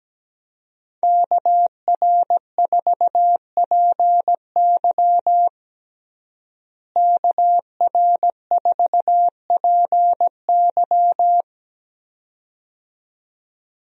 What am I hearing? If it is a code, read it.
Morse code, heard as a keyed tone, about 17 wpm, 700 Hz. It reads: KR4PY KR4PY
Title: USA Call Signs - 5 Characters - Repeated 2x - 17wpm